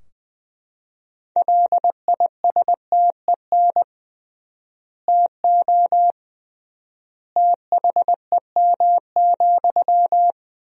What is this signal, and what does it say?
Morse code, heard as a keyed tone, about 20 wpm, 700 Hz. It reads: LISTEN TO THEM,
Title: 3x Word-Spacing - Sentences from Top 400 Words - 20wpm